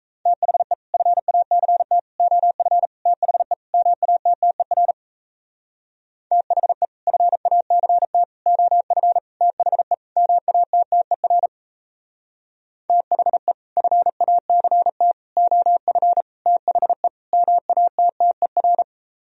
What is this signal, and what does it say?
Morse code, heard as a keyed tone, about 36 wpm, 700 Hz. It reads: THE FACT OF THE MATTER THE FACT OF THE MATTER THE FACT OF THE MATTER